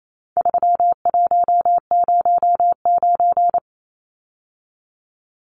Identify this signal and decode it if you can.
Morse code, heard as a keyed tone, about 28 wpm, 700 Hz. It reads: 3109
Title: Numbers - 4-Digit 28wpm